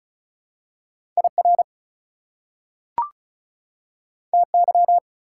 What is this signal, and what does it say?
Morse code, heard as a keyed tone, about 36 wpm, 690 Hz. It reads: IR TY
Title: Prefixes and Suffixes 35wpm